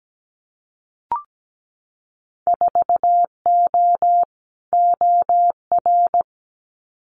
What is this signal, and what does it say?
Morse code, heard as a keyed tone, about 17 wpm, 700 Hz. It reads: 4O OR
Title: Winter Field Day Exchanges Encore 17wpm